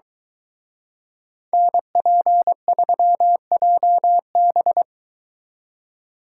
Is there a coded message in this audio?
Yes: Morse code from 1.5 to 4.8 s, heard as a keyed tone, about 22 wpm, 700 Hz. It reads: NP3JB